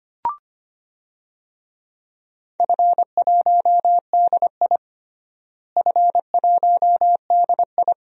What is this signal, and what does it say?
Morse code, heard as a keyed tone, about 25 wpm, 700 Hz. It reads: F1DI F1DI
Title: DXCC-100 Call Signs - 4 Characters - Repeated 3x - 25wpm